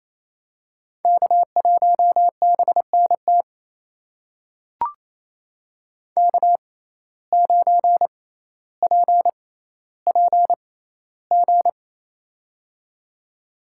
Morse code, heard as a keyed tone, about 28 wpm, spaced as for 14 wpm, 700 Hz. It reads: K1BNT K9PPG
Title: USA Call Signs - 5 Characters - 6x Character Spacing - 28wpm